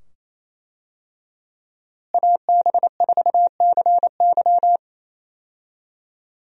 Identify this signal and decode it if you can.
Morse code, heard as a keyed tone, about 28 wpm, 700 Hz. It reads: AB4CY
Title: USA Call Signs - 5 Characters - Repeated 2x - 28wpm